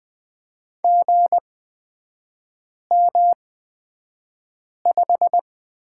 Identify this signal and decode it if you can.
Morse code, heard as a keyed tone, about 20 wpm, spaced as for 5 wpm, 700 Hz. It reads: GM5
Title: 224 Lesson 36e - BT - Three Characters 20wpm